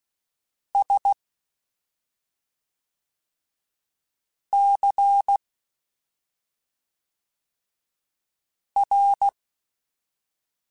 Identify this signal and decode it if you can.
Morse code, heard as a keyed tone, about 16 wpm, 800 Hz. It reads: S C R
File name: Special INGs / Special INGs-mp3s / SlowToFaster ing @5